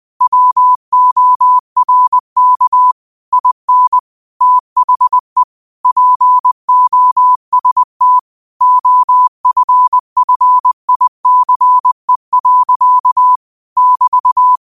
Morse code, heard as a keyed tone, about 20 wpm, 1000 Hz. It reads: WORK IN THE POST OFFICE. <BT>